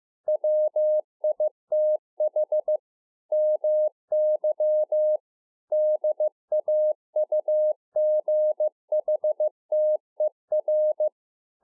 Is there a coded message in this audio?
Yes: Morse code from 0.3 to 11.1 s, heard as a keyed tone, about 15 wpm, 610 Hz. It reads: WITH MY DAUGHTER